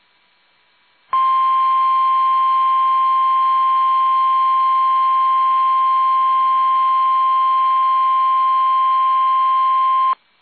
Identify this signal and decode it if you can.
Emergency Alert System (SAME) tones (the 1050 Hz attention tone). It is heard from 1.1 to 10.2 s.
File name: eas / KUSO / KUSO